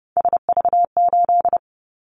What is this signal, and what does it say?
Morse code, heard as a keyed tone, about 30 wpm, 700 Hz. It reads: SV8
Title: Callsign Prefixes for Top 50 DXCC - 30wpm